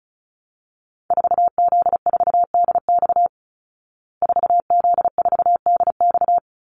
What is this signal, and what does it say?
Morse code, heard as a keyed tone, about 34 wpm, 700 Hz. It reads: 4Z4DX 4Z4DX
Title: Call Signs Repeated 3x - 35wpm